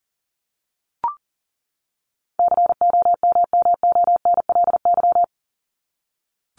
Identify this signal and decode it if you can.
Morse code, heard as a keyed tone, about 39 wpm, 700 Hz. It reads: COMMONLY
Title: LY Suffix - Top 100 Words with Base Words 40wpm